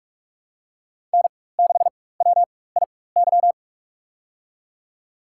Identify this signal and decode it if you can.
Morse code, heard as a keyed tone, about 46 wpm, spaced as for 21 wpm, 700 Hz. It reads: N6WIY
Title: USA Call Signs - 4x Character Spacing - 45wpm